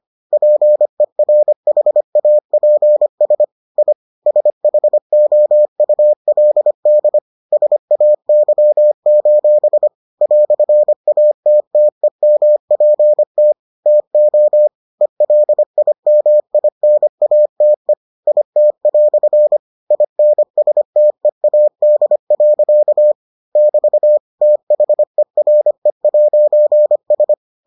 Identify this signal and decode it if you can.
Morse code, heard as a keyed tone, about 25 wpm, 600 Hz. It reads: PERHAPS I SHOULD SAY: "ATTEMPT TO ELIMINATE IT" INSTEAD. <BT> THERE'S